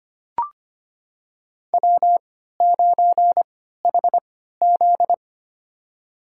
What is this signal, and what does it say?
Morse code, heard as a keyed tone, about 25 wpm, spaced as for 14 wpm, 700 Hz. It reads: W9HZ
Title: USA Call Signs - 4 Characters - 3x Character Spacing - 25wpm